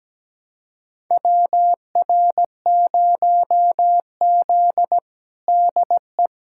Morse code, heard as a keyed tone, about 17 wpm, 700 Hz. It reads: WR0Z DE
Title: USA Callsign DE Callsign - 4 Characters - 17wpm